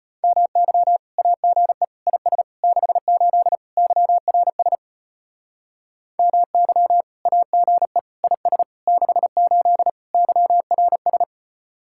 Morse code, heard as a keyed tone, about 33 wpm, 700 Hz. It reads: MY AGE IS 68 YRS MY AGE IS 68 YRS